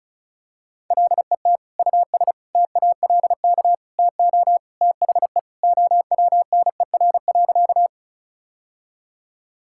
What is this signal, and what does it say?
Morse code, heard as a keyed tone, about 36 wpm, 700 Hz. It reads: LET US TALK TO THE OWNER.
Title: Sentences from Top 1600 Words 35wpm